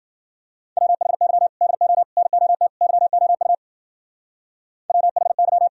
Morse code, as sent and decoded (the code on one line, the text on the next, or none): .-- .... -.-- / -.. --- / -. --- - / -.-- --- ..- / .-- .... -.--
WHY DO NOT YOU WHY